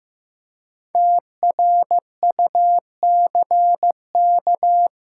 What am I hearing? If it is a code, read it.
Morse code, heard as a keyed tone, about 15 wpm, 700 Hz. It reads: TRUCK
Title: Practice QSO - Exchange 3 - More Info - 15wpm